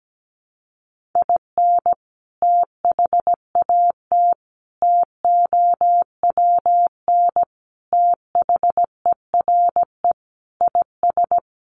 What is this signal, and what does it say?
Morse code, heard as a keyed tone, about 17 wpm, 700 Hz. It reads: IN THAT TOWN THERE IS
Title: Sentences from Top 500 Words 17wpm